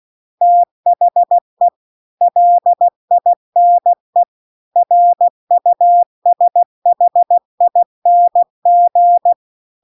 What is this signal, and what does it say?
Morse code, heard as a keyed tone, about 16 wpm, 700 Hz. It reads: THE LINE RUSHING